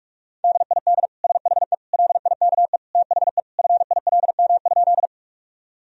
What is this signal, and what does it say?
Morse code, heard as a keyed tone, about 46 wpm, 700 Hz. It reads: DID SHE LIKE THE FILM?
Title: Sentences from Top 1000 Words - Review Part 3 - 45wpm